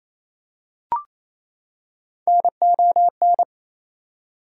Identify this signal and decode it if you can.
Morse code, heard as a keyed tone, about 28 wpm, 700 Hz. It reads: NON
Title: Prefixes and Suffixes 28wpm